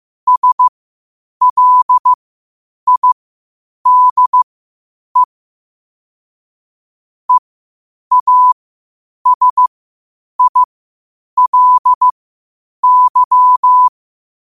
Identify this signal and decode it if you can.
Morse code, heard as a keyed tone, about 15 wpm, spaced as for 8 wpm, 1000 Hz. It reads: SLIDE EASILY
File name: qotdfiles / 8wpm